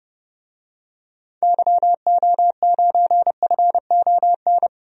Morse code, heard as a keyed tone, about 32 wpm, 700 Hz. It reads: YO9FON
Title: DXCC-100 Call Signs - 6 Characters - 30wpm